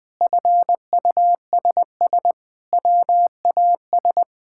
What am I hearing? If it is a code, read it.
Morse code, heard as a keyed tone, about 20 wpm, 700 Hz. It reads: FUSS WAS
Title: Sentences with 8-Word Phrases - 20wpm